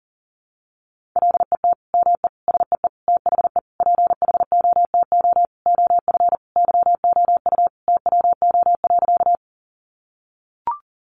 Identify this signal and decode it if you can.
Morse code, heard as a keyed tone, about 40 wpm, 700 Hz. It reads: LET ME SEE THE PHOTO OF YOU TWO.
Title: Sentences from Top 1200 Words 40wpm